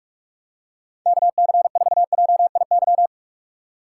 Morse code, heard as a keyed tone, about 44 wpm, 700 Hz. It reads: KC4JIY
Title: USA Call Signs - 6 Characters - 4x Character Spacing - 45wpm